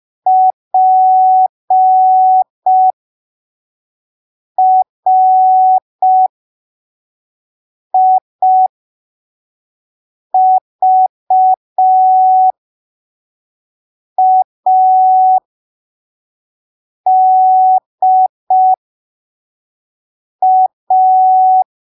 Morse code, heard as a keyed tone, about 5 wpm, spaced as for 3 wpm, 750 Hz. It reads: PRIVADA